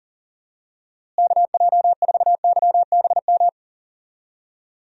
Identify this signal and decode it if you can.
Morse code, heard as a keyed tone, about 39 wpm, 700 Hz. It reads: KJ4YBM
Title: USA Call Signs - 6 Characters - Repeated 2x - 40wpm